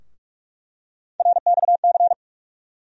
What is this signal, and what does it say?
Morse code, heard as a keyed tone, about 46 wpm, 700 Hz. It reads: RXC
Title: USA Call Sign Postfixes - 3 Letters 45wpm